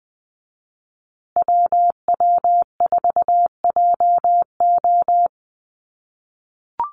Morse code, heard as a keyed tone, about 20 wpm, 700 Hz. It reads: WW4JO
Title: USA Call Signs Repeated 2x 20wpm